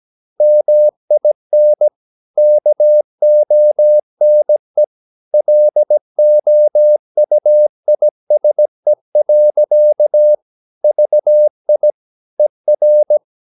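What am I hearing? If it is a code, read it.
Morse code, heard as a keyed tone, about 17 wpm, 600 Hz. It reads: MIN KONE LOUISE. VI ER
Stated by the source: Slow (QRS) Morse code at 16 wpm, from CW med Gnister.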